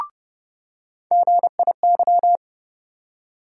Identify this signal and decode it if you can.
Morse code, heard as a keyed tone, about 30 wpm, 700 Hz. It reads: GIY
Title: USA Call Sign Postfixes - 3 Letters 30wpm